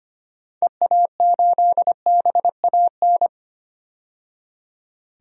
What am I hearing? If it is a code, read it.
Morse code, heard as a keyed tone, about 25 wpm, 700 Hz. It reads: EA8BAN
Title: DXCC-100 Call Signs - 6 Characters - 25wpm